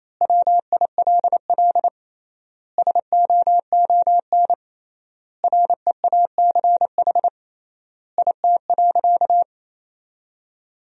Morse code, heard as a keyed tone, about 28 wpm, 700 Hz. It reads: WILL SOON REACH IT.
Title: 3x Word-Spacing - Sentences from Top 400 Words - 28wpm